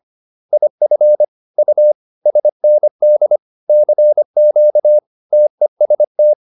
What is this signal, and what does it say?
Morse code, heard as a keyed tone, about 25 wpm, 600 Hz. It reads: IF U SND CQ TEST
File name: SS220 25